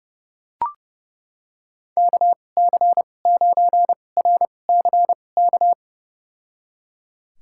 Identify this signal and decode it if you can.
Morse code, heard as a keyed tone, about 30 wpm, spaced as for 22 wpm, 700 Hz. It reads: KC9RCK
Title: USA Call Signs - 6 Characters - 2x Character Spacing - 30wpm